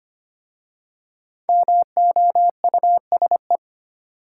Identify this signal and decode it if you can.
Morse code, heard as a keyed tone, about 25 wpm, 700 Hz. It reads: MOUSE
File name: 5 Letter Words - First 3 Letters Given - 25wpm